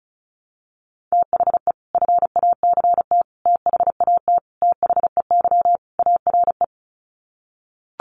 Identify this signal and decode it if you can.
Morse code, heard as a keyed tone, about 34 wpm, 700 Hz. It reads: THE FACT THAT THEY ARE
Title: Sets of 5 Words Encore2 35wpm